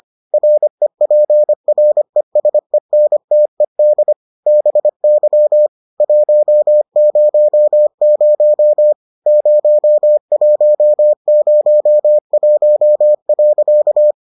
Morse code, heard as a keyed tone, about 25 wpm, 600 Hz. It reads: REPRESENTED BY 100 0101.